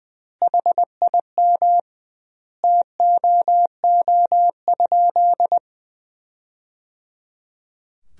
Morse code, heard as a keyed tone, about 20 wpm, 700 Hz. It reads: HIM TOO?
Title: 2x Word-Spacing - Sentences from Top 500 Words - 20wpm